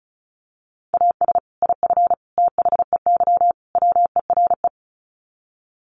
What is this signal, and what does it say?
Morse code, heard as a keyed tone, about 34 wpm, 700 Hz. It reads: AS IF THEY WERE
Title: Sets of 4 Words 35wpm